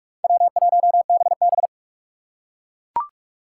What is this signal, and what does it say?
Morse code, heard as a keyed tone, about 43 wpm, 700 Hz. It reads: W1BB
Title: USA Call Signs - 4 Characters - 3x Character Spacing - 45wpm